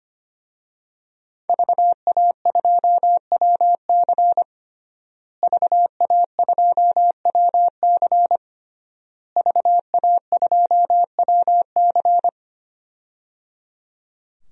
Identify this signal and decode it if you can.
Morse code, heard as a keyed tone, about 25 wpm, 700 Hz. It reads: VA2WC VA2WC VA2WC
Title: DXCC-100 Call Signs - 5 Characters - Repeated 3x - 25wpm